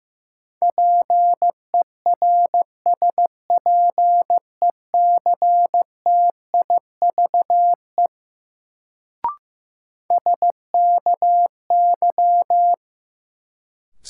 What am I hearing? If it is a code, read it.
Morse code, heard as a keyed tone, about 15 wpm, 700 Hz. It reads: PERSPECTIVE SKY
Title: Sentences from Top 1200 Words - New Word Drill Alt - 15wpm